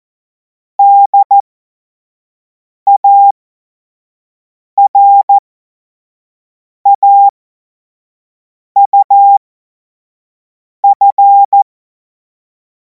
Morse code, heard as a keyed tone, about 14 wpm, spaced as for 5 wpm, 800 Hz. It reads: DARAUF
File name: Morsekurs / Witze / Witz 02. Juni 2026 25 F70